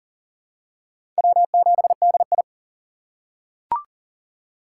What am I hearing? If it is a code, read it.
Morse code, heard as a keyed tone, about 37 wpm, 690 Hz. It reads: W7DI
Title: USA Call Signs - 3x Character Spacing - 40wpm